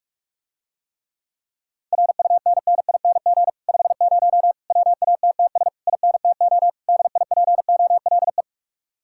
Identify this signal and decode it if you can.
Morse code, heard as a keyed tone, about 45 wpm, 700 Hz. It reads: RUNNING 50 WATTS INTO DIPOLE